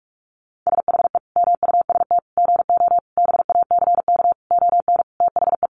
Morse code, heard as a keyed tone, about 44 wpm, 700 Hz. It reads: SHE MUST GO BACK ON THE
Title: Sentences from Top 1200 Words 45wpm